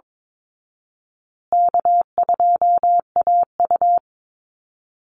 Morse code, heard as a keyed tone, about 22 wpm, 700 Hz. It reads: K2AU
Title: Easy Call Signs Repeated 3x - 22wpm